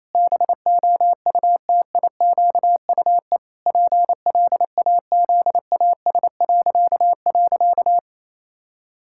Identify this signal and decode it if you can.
Morse code, heard as a keyed tone, about 28 wpm, 700 Hz. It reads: BOUTIQUE PLAZAS..
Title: Pangram Challenge v2 - 28wpm